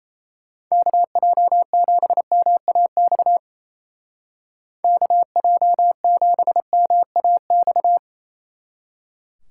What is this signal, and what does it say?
Morse code, heard as a keyed tone, about 37 wpm, 700 Hz. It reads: KJ7MAX KJ7MAX